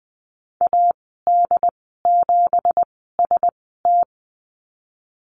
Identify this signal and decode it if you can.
Morse code, heard as a keyed tone, about 20 wpm, 700 Hz. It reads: A D 7 S T
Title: 231 Lesson 37g - Letter x - Call Signs 20wpm